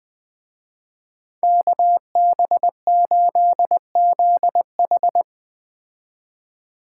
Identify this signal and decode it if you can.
Morse code, heard as a keyed tone, about 20 wpm, 700 Hz. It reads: KB8ZH
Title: Speed Racing - USA Call Signs 17wpm